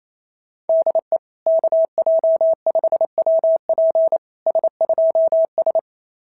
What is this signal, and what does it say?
Morse code, heard as a keyed tone, about 27 wpm, 650 Hz. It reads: DE KJ5WP S2S
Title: SOTA QSO Practice 28wpm